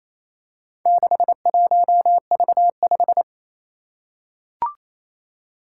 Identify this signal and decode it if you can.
Morse code, heard as a keyed tone, about 28 wpm, 700 Hz. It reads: 61V5
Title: Four Challenging Letter-Number Combinations 28wpm